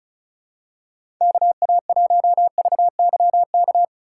Morse code, heard as a keyed tone, about 36 wpm, 700 Hz. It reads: KA1VYK
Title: USA Call Signs - 6 Characters - 3x Character Spacing - 35wpm